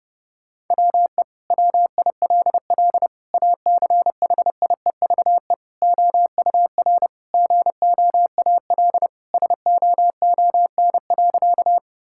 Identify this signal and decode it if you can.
Morse code, heard as a keyed tone, about 30 wpm, 700 Hz. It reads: WE WILL ACHIEVE OUR GOAL SOON.
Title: Sentences from Top 1000 Words - Review Part 6 - 30wpm